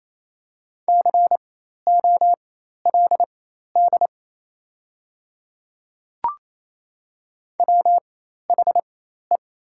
Morse code, heard as a keyed tone, about 28 wpm, spaced as for 13 wpm, 700 Hz. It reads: COLD WHE
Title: 4x Character Spacing - Top 500 Words - 28wpm